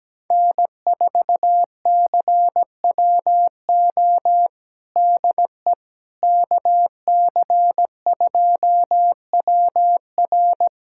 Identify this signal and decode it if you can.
Morse code, heard as a keyed tone, about 17 wpm, 700 Hz. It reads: N4CWO DE KC2WR
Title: USA Callsign DE Callsign - 5 Characters - 17wpm